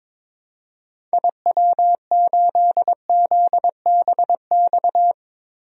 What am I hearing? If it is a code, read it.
Morse code, heard as a keyed tone, about 22 wpm, 700 Hz. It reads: IW8ZBX
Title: DXCC-100 Call Signs - 6 Characters - 22wpm